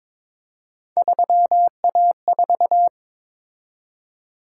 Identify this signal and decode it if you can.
Morse code, heard as a keyed tone, about 22 wpm, 700 Hz. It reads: 3A4
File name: practice / 22wpm / Three Letter-Number Combinations - 22wpm